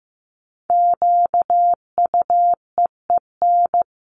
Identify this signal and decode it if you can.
Morse code, heard as a keyed tone, about 15 wpm, 700 Hz. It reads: QUEEN
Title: A Sound Of Thunder 15wpm